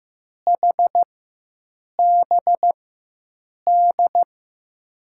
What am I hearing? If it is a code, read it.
Morse code, heard as a keyed tone, about 15 wpm, spaced as for 7 wpm, 700 Hz. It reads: HBD
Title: USA Call Signs - 6 Characters - 4x Character Spacing - 15wpm